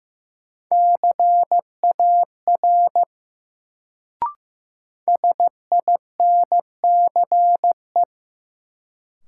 Morse code, heard as a keyed tone, about 15 wpm, 700 Hz. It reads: CAR SINCE
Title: Sentences from Top 300 Words - New Word Drill - 15wpm